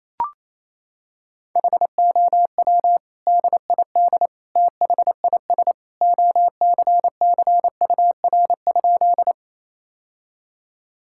Morse code, heard as a keyed tone, about 28 wpm, 700 Hz. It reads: HOW DID THIS OCCUR?
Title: Sentences from Top 1000 Words - Review Part 6 - 28wpm